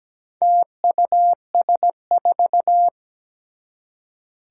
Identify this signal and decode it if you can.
Morse code, heard as a keyed tone, about 17 wpm, 700 Hz. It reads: TUS4
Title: Four Challenging Letter-Number Combinations 17wpm